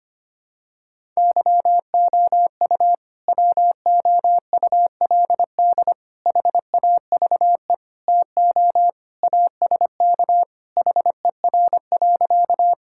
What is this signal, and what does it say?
Morse code, heard as a keyed tone, about 25 wpm, 700 Hz. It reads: YOU WOULD HAVE TO ASK HER.
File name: practice / 25wpm / Sentences with 3-Word Phrases - 25wpm